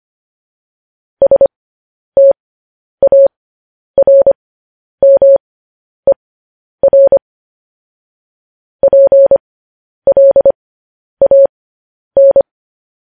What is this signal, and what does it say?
Morse code, heard as a keyed tone, about 25 wpm, spaced as for 10 wpm, 550 Hz. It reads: STARMER PLAN
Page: news-headlines-in-morse-code-at-10-wpm